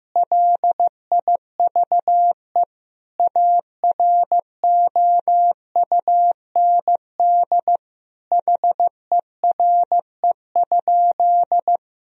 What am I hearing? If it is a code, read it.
Morse code, heard as a keyed tone, about 15 wpm, 700 Hz. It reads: LIVE AROUND HERE?
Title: Sentences from Top 1000 Words - Review Part 3 - 15wpm